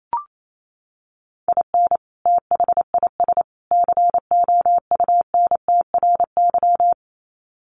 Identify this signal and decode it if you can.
Morse code, heard as a keyed tone, about 28 wpm, 700 Hz. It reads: IN THIS COUNTRY
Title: Sets of 3 Words 28wpm